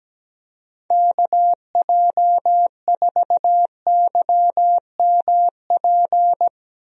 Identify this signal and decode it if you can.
Morse code, heard as a keyed tone, about 17 wpm, 700 Hz. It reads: KJ4YMP
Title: USA Call Signs - 6 Characters - Repeated 3x - 17wpm